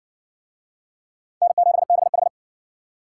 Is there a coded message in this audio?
Yes: Morse code from 1.4 to 2.3 s, heard as a keyed tone, about 53 wpm, 700 Hz. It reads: N7BH